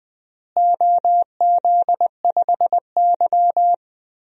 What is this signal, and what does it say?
Morse code, heard as a keyed tone, about 20 wpm, 700 Hz. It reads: OZ5Y
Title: DXCC-100 Call Signs - 4 Characters - Repeated 2x - 20wpm